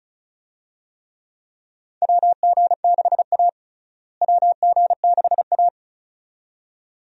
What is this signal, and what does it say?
Morse code, heard as a keyed tone, about 34 wpm, 700 Hz. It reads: WG6A WG6A
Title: USA Call Signs - 4 Characters - Repeated 2x - 35wpm